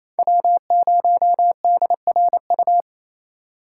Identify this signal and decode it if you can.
Morse code, heard as a keyed tone, about 28 wpm, 700 Hz. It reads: W0DRU
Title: USA Call Signs - 4x Character Spacing - 28wpm